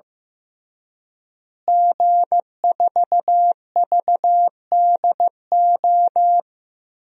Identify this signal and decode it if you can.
Morse code, heard as a keyed tone, about 15 wpm, 700 Hz. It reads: G4VDO
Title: DXCC-100 Call Signs - 5 Characters - Repeated 3x - 15wpm